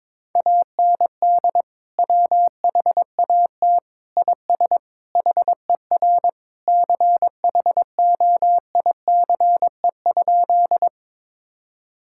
Morse code, heard as a keyed tone, about 22 wpm, 700 Hz. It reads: AND WHAT IS HER CHOICE?
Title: Sentences from Top 1000 Words - Review Part 6 - 22wpm